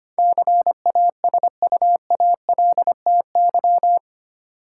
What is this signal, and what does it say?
Morse code, heard as a keyed tone, about 25 wpm, 700 Hz. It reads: CASUALTY
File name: Eight Letter Words - 25wpm